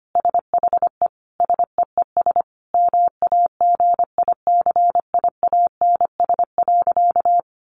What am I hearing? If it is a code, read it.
Morse code, heard as a keyed tone, about 25 wpm, 700 Hz. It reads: SHE SEES MAGICIANS.
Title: Easy Phrases 25wpm